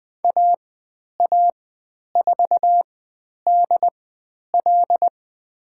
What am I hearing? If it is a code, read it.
Morse code, heard as a keyed tone, about 20 wpm, spaced as for 10 wpm, 700 Hz. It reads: AA4DL
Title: 095 Lesson 16g - Letter c - Call Signs 20wpm